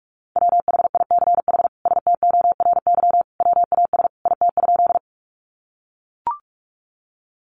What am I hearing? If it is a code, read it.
Morse code, heard as a keyed tone, about 44 wpm, 700 Hz. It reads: WHICH STORY WAS IT?
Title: Sentences from Top 300 Words 45wpm